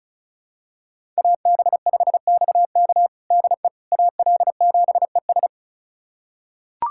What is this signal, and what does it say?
Morse code, heard as a keyed tone, about 36 wpm, 700 Hz. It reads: AB5XK DE AL7ES
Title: USA Callsign DE Callsign - 5 Characters - 35wpm